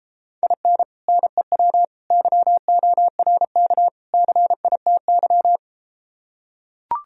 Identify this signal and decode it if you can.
Morse code, heard as a keyed tone, about 33 wpm, 700 Hz. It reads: IN NEW YORK CITY